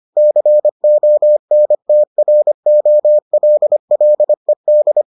Morse code, heard as a keyed tone, about 25 wpm, 600 Hz. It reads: CONTROLLED